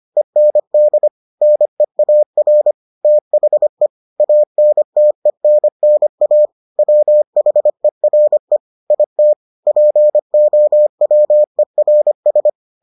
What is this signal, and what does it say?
Morse code, heard as a keyed tone, about 25 wpm, 600 Hz. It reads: END NEAR THE ANTENNA WHERE IT POWERS